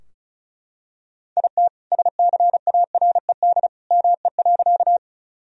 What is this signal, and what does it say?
Morse code, heard as a keyed tone, about 34 wpm, 700 Hz. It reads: IT SCARED ME.
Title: Sentences from Top 1000 Words - Review Part 4 - 35wpm